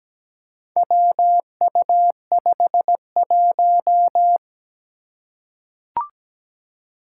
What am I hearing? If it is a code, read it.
Morse code, heard as a keyed tone, about 17 wpm, 700 Hz. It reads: WU51
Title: Four Challenging Letter-Number Combinations 17wpm